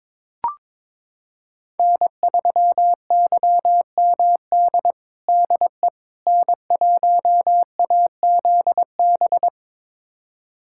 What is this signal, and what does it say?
Morse code, heard as a keyed tone, about 22 wpm, 700 Hz. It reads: N3YMD DE N1AZB
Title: USA Callsign DE Callsign - 5 Characters - 22wpm